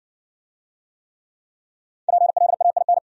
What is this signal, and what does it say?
Morse code, heard as a keyed tone, about 60 wpm, 700 Hz. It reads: PLAIN